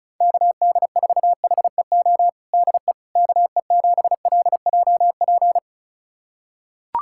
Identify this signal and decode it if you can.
Morse code, heard as a keyed tone, about 36 wpm, 700 Hz. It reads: KD4HEO DE KE7LJP
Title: USA Callsign DE Callsign 35wpm